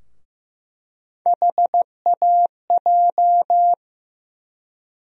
Morse code, heard as a keyed tone, about 15 wpm, 700 Hz. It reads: HAJ
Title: USA Call Sign Postfixes - 3 Letters 15wpm